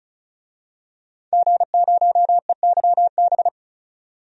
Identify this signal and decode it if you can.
Morse code, heard as a keyed tone, about 36 wpm, 700 Hz. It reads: G0EYB
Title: DXCC-100 Call Signs - 5 Characters - 35wpm